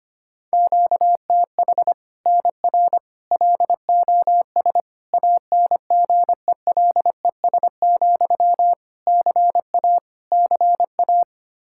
Morse code, heard as a keyed tone, about 25 wpm, 700 Hz. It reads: QTH NR LOS ANGELES, CA CA
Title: Practice QSO - Exchange 1 - Beginning - 25wpm